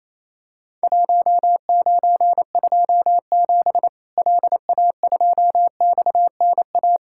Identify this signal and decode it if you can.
Morse code, heard as a keyed tone, about 28 wpm, 700 Hz. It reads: 1927 LA2XNA
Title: Worked All Europe DX Contest - QTC Exchange - 28wpm